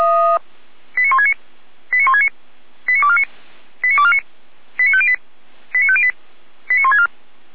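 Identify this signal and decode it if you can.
Five-tone selective call (ZVEI), heard as quick runs of five tones, tones of about 70 ms. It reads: 89169 89169 89249 89249 89508 89508 89164
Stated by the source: ZVEI five-tone selective call, code 89167, received by SDR; recording hosted by feuerwehr-orlamuende.org (fire brigade Orlamünde).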